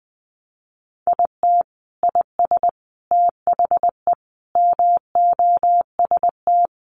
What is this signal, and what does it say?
Morse code, heard as a keyed tone, about 20 wpm, 700 Hz. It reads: IT IS THE MOST
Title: Sentences from Top 1500 Words 20wpm